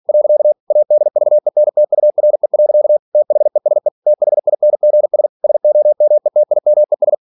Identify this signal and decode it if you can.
Morse code, heard as a keyed tone, about 46 wpm, 600 Hz. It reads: . ADVENTURE. THESE THINGS SOMETIMES